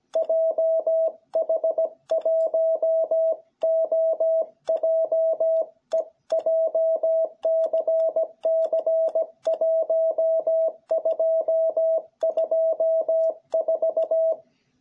Morse code, heard as a keyed tone, about 17 wpm, 660 Hz. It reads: JH1OJE JCC1224